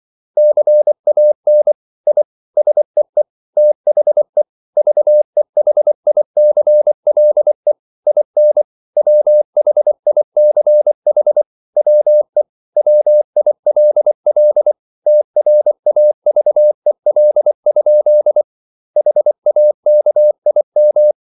Morse code, heard as a keyed tone, about 24 wpm, 600 Hz. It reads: CAN I SEE THE VEHICLE IN WHICH WE WILL TRAVEL? HAKIM